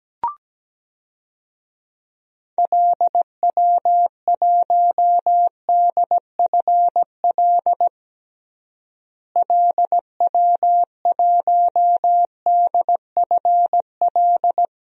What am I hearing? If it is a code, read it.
Morse code, heard as a keyed tone, about 17 wpm, 700 Hz. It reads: LW1DFL LW1DFL
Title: DXCC-100 Call Signs - 6 Characters - Repeated 3x - 17wpm